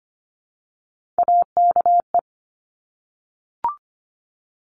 Morse code, heard as a keyed tone, about 25 wpm, 700 Hz. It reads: AKE
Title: Top Three Letter Combinations in Words 25wpm